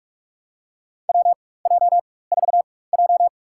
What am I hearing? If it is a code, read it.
Morse code, heard as a keyed tone, about 46 wpm, spaced as for 21 wpm, 690 Hz. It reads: WJ4J
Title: USA Call Signs - 4x Character Spacing - 45wpm